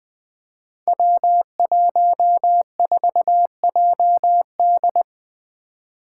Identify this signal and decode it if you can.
Morse code, heard as a keyed tone, about 20 wpm, 700 Hz. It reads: W14JD
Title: Five Challenging Letter-Number Combinations 20wpm